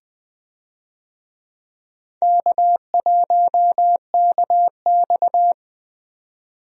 Morse code, heard as a keyed tone, about 20 wpm, 700 Hz. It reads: K1KX
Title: USA Call Signs - 4 Characters - 20wpm